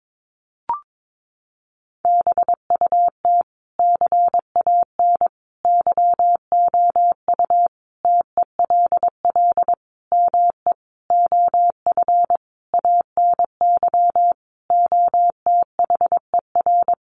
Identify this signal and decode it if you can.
Morse code, heard as a keyed tone, about 22 wpm, 700 Hz. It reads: BUT CAN YOU TELL ME OF ANY OTHER
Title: Sentences from Top 1400 Words 22wpm